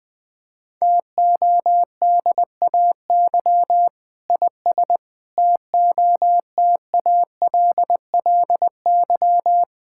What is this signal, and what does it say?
Morse code, heard as a keyed tone, about 20 wpm, 700 Hz. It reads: TODAY IS TOTALLY
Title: Sentences from Top 1600 Words 20wpm